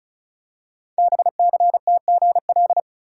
Morse code, heard as a keyed tone, about 36 wpm, 700 Hz. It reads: BCTGL
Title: Code Groups - 5 Letters - 35wpm